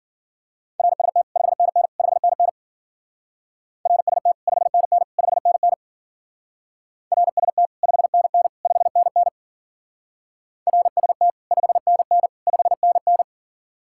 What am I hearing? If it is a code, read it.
Morse code, heard as a keyed tone, about 48 wpm, 700 Hz. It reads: RST 5NN 5NN RST 5NN 5NN RST 5NN 5NN RST 5NN 5NN
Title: Speed-Racing Common QSO Phrases - 40wpm